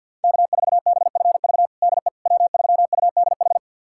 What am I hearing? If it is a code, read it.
Morse code, heard as a keyed tone, about 52 wpm, 700 Hz. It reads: K4BFV DE W3UDH
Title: USA Callsign DE Callsign - 5 Characters - 50wpm